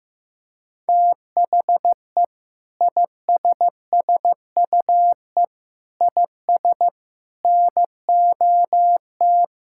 Morse code, heard as a keyed tone, about 15 wpm, 700 Hz. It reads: THE ISSUE IS NOT
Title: Sentences from Top 1000 Words - Review Part 6 - 15wpm